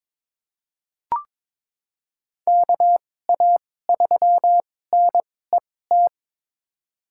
Morse code, heard as a keyed tone, about 21 wpm, spaced as for 16 wpm, 700 Hz. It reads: KA3NET
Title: USA Call Signs - 6 Characters - 2x Character Spacing - 22wpm